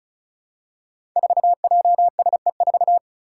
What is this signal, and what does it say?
Morse code, heard as a keyed tone, about 36 wpm, 700 Hz. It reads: 4JSE4
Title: Five Challenging Letter-Number Combinations 35wpm